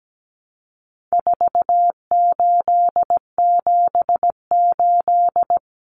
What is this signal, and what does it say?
Morse code, heard as a keyed tone, about 17 wpm, 700 Hz. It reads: 4878
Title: Common Cut Numbers - 4-Digit 17wpm